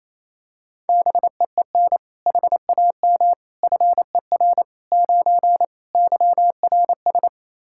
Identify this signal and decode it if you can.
Morse code, heard as a keyed tone, about 28 wpm, 700 Hz. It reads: BEEN HAM FER 9 YRS
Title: Practice QSO - Exchange 3 - More Info - 28wpm